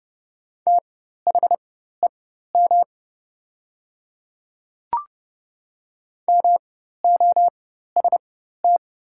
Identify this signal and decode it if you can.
Morse code, heard as a keyed tone, about 30 wpm, spaced as for 14 wpm, 690 Hz. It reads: THEM MOST
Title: 4x Character Spacing - Top 100 Words - 30wpm